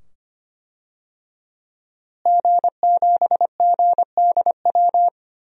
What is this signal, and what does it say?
Morse code, heard as a keyed tone, about 25 wpm, 700 Hz. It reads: G7GDW
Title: DXCC-100 Call Signs - 5 Characters - Repeated 2x - 25wpm